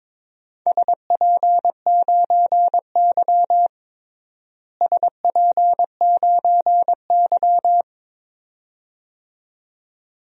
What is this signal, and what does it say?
Morse code, heard as a keyed tone, about 22 wpm, 700 Hz. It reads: SP9Y SP9Y
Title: DXCC-100 Call Signs - 4 Characters - Repeated 2x - 22wpm